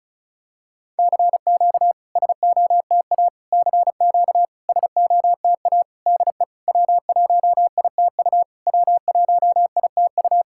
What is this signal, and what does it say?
Morse code, heard as a keyed tone, about 36 wpm, 690 Hz. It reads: CQ SOTA CQ SOTA DE W1ITU W1ITU
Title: SOTA QSO Practice 35wpm